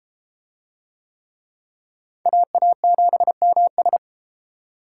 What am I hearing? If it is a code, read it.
Morse code, heard as a keyed tone, about 30 wpm, 700 Hz. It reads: AA7MS